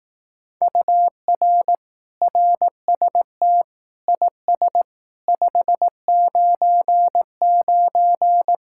Morse code, heard as a keyed tone, about 18 wpm, 700 Hz. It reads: UR RST IS 599
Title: Speed-Racing Common QSO Phrases Encore - 15wpm